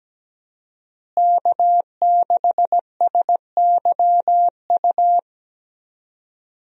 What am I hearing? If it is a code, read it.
Morse code, heard as a keyed tone, about 17 wpm, 700 Hz. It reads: K6SYU
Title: USA Call Signs - 5 Characters - Repeated 2x - 17wpm